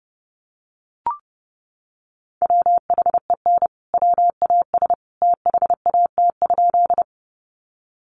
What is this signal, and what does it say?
Morse code, heard as a keyed tone, about 30 wpm, 700 Hz. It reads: WHEN WAS THAT?
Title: Sentences from Top 100 Words 30wpm